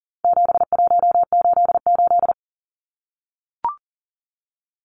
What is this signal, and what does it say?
Morse code, heard as a keyed tone, about 40 wpm, 700 Hz. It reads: 7188
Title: Numbers - 4-Digit 40wpm